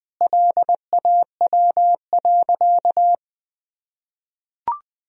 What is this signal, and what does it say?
Morse code, heard as a keyed tone, about 19 wpm, 700 Hz. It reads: LAW.
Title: Sentences from Top 1000 Words - Review Part 3 - 20wpm